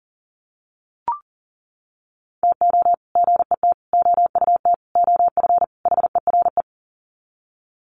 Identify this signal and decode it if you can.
Morse code, heard as a keyed tone, about 40 wpm, 700 Hz. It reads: TO GET OUT OF HERE
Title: Sets of 5 Words Encore 40wpm